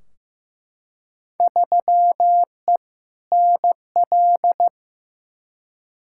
Morse code, heard as a keyed tone, about 15 wpm, 700 Hz. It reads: 3E NL
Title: Field Day Exchanges Encore - 15wpm